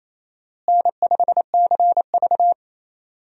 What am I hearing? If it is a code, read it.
Morse code, heard as a keyed tone, about 28 wpm, 700 Hz. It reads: N5CV
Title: USA Call Signs - 4 Characters - Repeated 3x - 28wpm